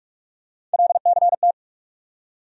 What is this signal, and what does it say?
Morse code, heard as a keyed tone, about 48 wpm, 700 Hz. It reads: LCT
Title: USA Call Sign Postfixes - 3 Letters 45wpm